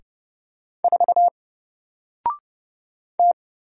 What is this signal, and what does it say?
Morse code, heard as a keyed tone, about 30 wpm, 700 Hz. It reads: 4 T
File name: Single-Letter-Number-30wpm